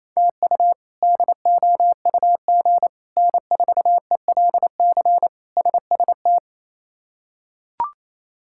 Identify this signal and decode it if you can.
Morse code, heard as a keyed tone, about 28 wpm, 700 Hz. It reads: TU DOUG N4ELC SST
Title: K1USN SST QSO Practice - 28wpm